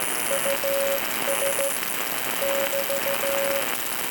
Morse code, heard as a keyed tone, about 15 wpm, 550 Hz, with noise. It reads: US <BT>